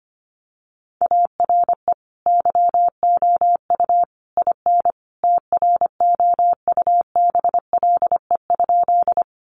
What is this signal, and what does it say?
Morse code, heard as a keyed tone, about 25 wpm, 700 Hz. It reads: ARE YOU IN TROUBLE?
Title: Sentences from Top 1200 Words 25wpm